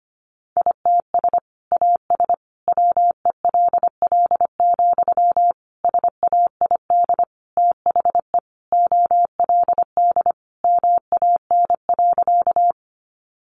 Morse code, heard as a keyed tone, about 25 wpm, 700 Hz. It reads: ITS AS WELL, SAID THE OLD MAN.
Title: Sentences from Top 600 Words 25wpm